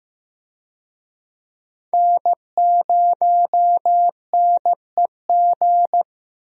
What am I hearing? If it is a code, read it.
Morse code, heard as a keyed tone, about 15 wpm, 700 Hz. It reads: N0NEG
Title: USA Call Signs - 5 Characters - Repeated 3x - 15wpm